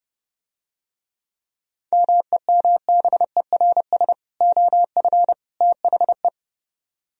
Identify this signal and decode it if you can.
Morse code, heard as a keyed tone, about 30 wpm, 700 Hz. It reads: MEMBERS OF THE